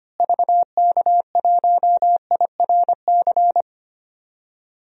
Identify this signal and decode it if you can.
Morse code, heard as a keyed tone, about 25 wpm, 700 Hz. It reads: VK1IRC
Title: DXCC-100 Call Signs - 6 Characters - Repeated 2x - 25wpm